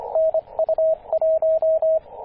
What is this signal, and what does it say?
Morse code, heard as a keyed tone, about 27 wpm, 640 Hz, with radio noise. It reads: N U 1
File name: t8cw-18-0737mhz-cw